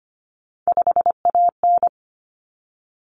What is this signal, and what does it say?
Morse code, heard as a keyed tone, about 25 wpm, 700 Hz. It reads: 5AN
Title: Common Cut Numbers - 3-Digit 25wpm